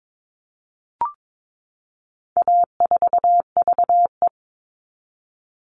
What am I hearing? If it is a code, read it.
Morse code, heard as a keyed tone, about 22 wpm, 700 Hz. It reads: A4VE
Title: All Cut Numbers - 4-Digit 22wpm